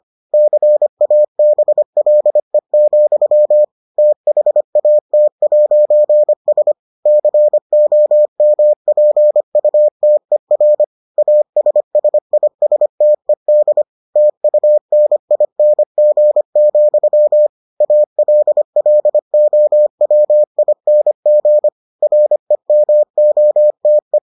Morse code, heard as a keyed tone, about 25 wpm, 600 Hz. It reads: CABLE, THAT'S COMPUTER ASSISTED TUNING, ALLOWING REMOTE